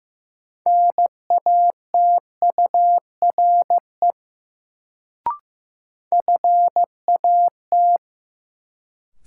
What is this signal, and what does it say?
Morse code, heard as a keyed tone, about 15 wpm, 700 Hz. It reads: NATURE FAT
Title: Sentences from Top 600 Words - New Word Drill - 15wpm